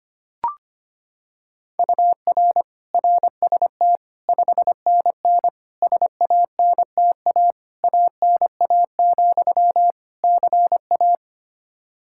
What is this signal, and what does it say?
Morse code, heard as a keyed tone, about 25 wpm, 700 Hz. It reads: UR RST 5NN SANTA ANA, CA
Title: Practice QSO - Exchange 1 - Beginning - 25wpm